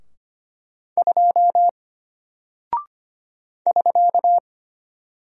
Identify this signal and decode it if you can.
Morse code, heard as a keyed tone, about 25 wpm, 700 Hz. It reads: 2 <SK>
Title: Single Character (Letter:Number:Punctuation:Prosign) Alt 25wpm